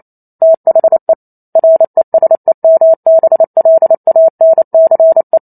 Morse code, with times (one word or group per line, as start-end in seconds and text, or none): 0.4-1.1 THE
1.6-5.4 RESEMBLANCE